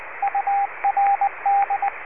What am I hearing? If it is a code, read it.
Morse code, heard as a keyed tone, about 20 wpm, 800 Hz, with radio noise. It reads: URD
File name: URD Leningrad replica 7035khz